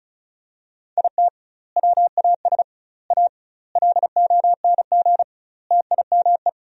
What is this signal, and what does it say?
Morse code, heard as a keyed tone, about 36 wpm, 700 Hz. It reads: IT WAS A LONG TIME
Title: 2x Word-Spacing - Sentences from Top 500 Words - 35wpm